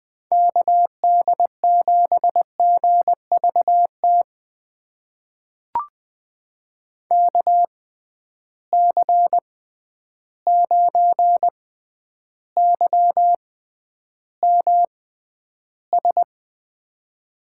Morse code, heard as a keyed tone, about 21 wpm, spaced as for 10 wpm, 700 Hz. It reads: KD7GVT KC9YMS
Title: USA Call Signs - 6x Character Spacing - 20wpm